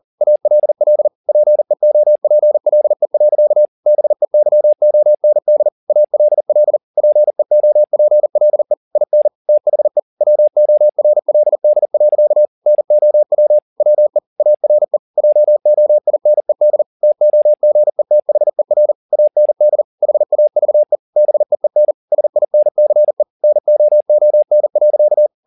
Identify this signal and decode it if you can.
Morse code, heard as a keyed tone, about 40 wpm, 600 Hz. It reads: ALL PEOPLE. BEYOND ALL PEOPLE IN THE WORLD. NOW WE ARE JOINED TOGETHER AND HAVE BEEN SINCE NOON.